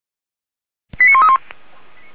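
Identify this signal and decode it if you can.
Five-tone selective call (ZVEI), heard as quick runs of five tones, tones of about 70 ms. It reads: 89131
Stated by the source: ZVEI five-tone selective call 89132 received by SDR, fire brigade Feuerwehr Orlamünde.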